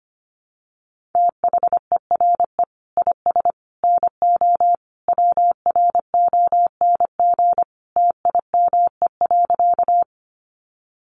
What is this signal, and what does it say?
Morse code, heard as a keyed tone, about 25 wpm, 700 Hz. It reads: THERE IS NO WRONG TIME.
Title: Sentences from Top 900 Words 25wpm